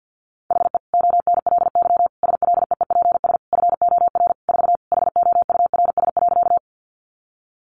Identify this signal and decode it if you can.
Morse code, heard as a keyed tone, about 50 wpm, 700 Hz. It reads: HE ONLY SLEEPS FOR 4 HOURS.
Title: Additional Easy Phrases 50wpm